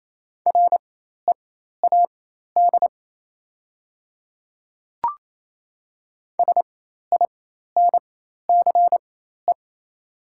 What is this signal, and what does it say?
Morse code, heard as a keyed tone, about 28 wpm, spaced as for 13 wpm, 700 Hz. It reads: READ SINCE
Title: 4x Character Spacing - Top 300 Words - 28wpm